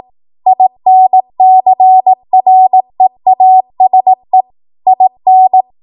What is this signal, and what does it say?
Morse code, heard as a keyed tone, about 18 wpm, 750 Hz. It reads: INCREASE IN
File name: Morse / 18WPM